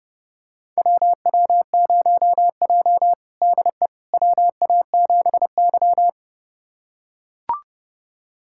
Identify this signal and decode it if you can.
Morse code, heard as a keyed tone, about 30 wpm, 700 Hz. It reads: WW0J DE WA7Y
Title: USA Callsign DE Callsign - 4 Characters - 30wpm